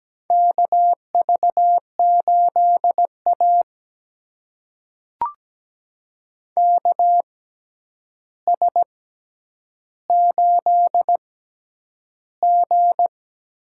Morse code, heard as a keyed tone, about 17 wpm, spaced as for 9 wpm, 700 Hz. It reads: KV8A KS8G
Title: USA Call Signs - 4 Characters - 6x Character Spacing - 17wpm